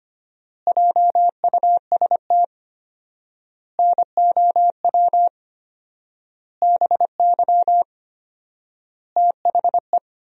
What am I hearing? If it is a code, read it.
Morse code, heard as a keyed tone, about 25 wpm, 700 Hz. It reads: JUST NOW BY THE
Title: 4x Word-Spacing - Sentences from Top 300 Words - 25wpm